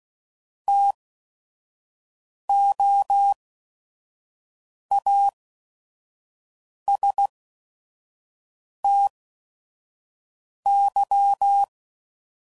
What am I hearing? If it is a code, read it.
Morse code, heard as a keyed tone, about 16 wpm, spaced as for 5 wpm, 800 Hz. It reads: TOASTY
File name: CW Code Groups @5WPM / Weather Sets / Weather 5 12